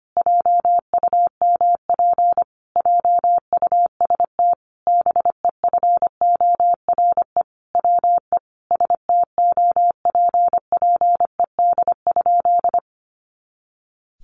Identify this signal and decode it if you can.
Morse code, heard as a keyed tone, about 25 wpm, 700 Hz. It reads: JUMP JUST BEFORE WE STOPPED?
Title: A Sound Of Thunder 25wpm